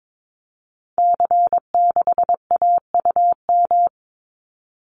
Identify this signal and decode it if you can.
Morse code, heard as a keyed tone, about 22 wpm, 700 Hz. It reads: C6AUM
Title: Call Signs Repeated 3x - 22wpm